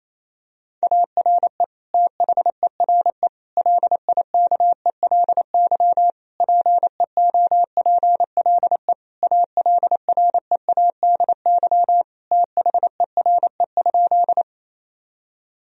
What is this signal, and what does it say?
Morse code, heard as a keyed tone, about 28 wpm, 700 Hz. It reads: ARE THERE LIKELY PEOPLE ALREADY THERE?
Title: Sentences from Top 1000 Words - Review Part 4 - 28wpm